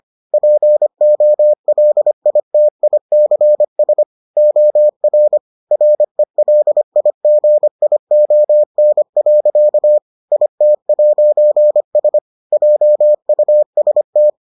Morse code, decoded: POLITICS OR RELIGION. IT'S JUST

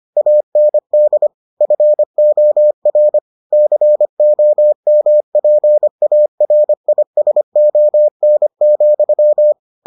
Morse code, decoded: AND FOR COMPARISON,

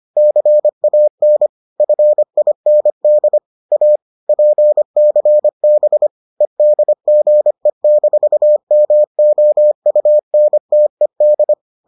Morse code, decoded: CAN FIND A PCB EDGE-MOUNTED

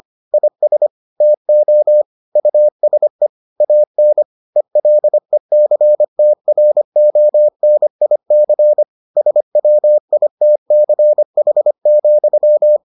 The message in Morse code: IS TO USE AN ELECTRONIC SWITCH,

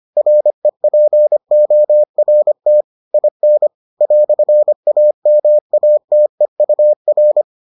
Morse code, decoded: REPORT IN "AMATEUR

